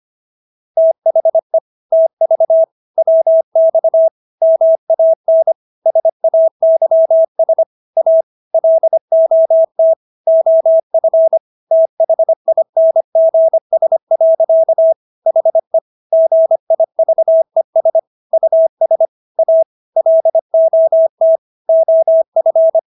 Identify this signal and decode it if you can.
Morse code, heard as a keyed tone, about 25 wpm, 650 Hz. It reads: THE TV WX MAN SAYS A LOT OF THINGS. HE GIVES US A LOT OF